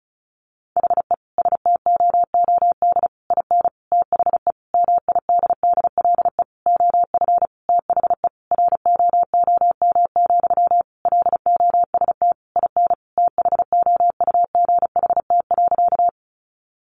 Morse code, heard as a keyed tone, about 34 wpm, 700 Hz. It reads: HE STOOD IN THE MIDDLE OF THE ROOM, LOST IN THOUGHT.
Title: Sentences from Top 1400 Words 35wpm